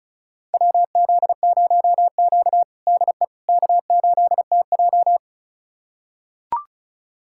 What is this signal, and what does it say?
Morse code, heard as a keyed tone, about 36 wpm, 700 Hz. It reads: WZ0Q DE K8TJ
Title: USA Callsign DE Callsign - 4 Characters - 35wpm